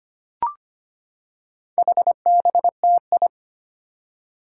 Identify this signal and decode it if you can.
Morse code, heard as a keyed tone, about 25 wpm, 700 Hz. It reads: HBTI
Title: Four Challenging Letter-Number Combinations 25wpm